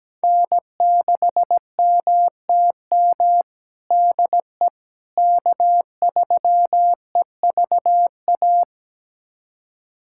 Morse code, heard as a keyed tone, about 17 wpm, 700 Hz. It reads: N6MTM DE K3EVA
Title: USA Callsign DE Callsign - 5 Characters - 17wpm